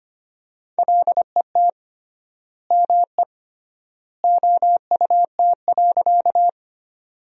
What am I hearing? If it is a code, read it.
Morse code, heard as a keyed tone, about 25 wpm, 700 Hz. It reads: LET ME OUT.
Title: 3x Word-Spacing - Sentences from Top 300 Words - 25wpm